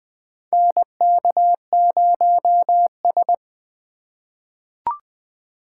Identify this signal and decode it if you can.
Morse code, heard as a keyed tone, about 20 wpm, 700 Hz. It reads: NK0S
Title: USA Call Signs - 4 Characters - 2x Character Spacing - 20wpm